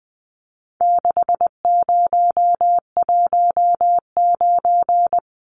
Morse code, heard as a keyed tone, about 20 wpm, 700 Hz. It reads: 6019
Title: Numbers - 4-Digit 20wpm